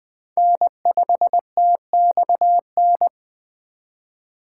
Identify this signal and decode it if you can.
Morse code, heard as a keyed tone, about 22 wpm, 700 Hz. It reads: N5TXN